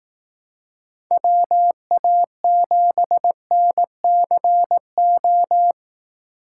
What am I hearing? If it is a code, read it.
Morse code, heard as a keyed tone, about 18 wpm, 700 Hz. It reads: WA7NCO
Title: Speed Racing - USA Call Signs 15wpm